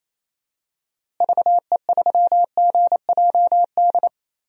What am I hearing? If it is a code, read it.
Morse code, heard as a keyed tone, about 28 wpm, 700 Hz. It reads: VE3GJD
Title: DXCC-100 Call Signs - 6 Characters - Repeated 2x - 28wpm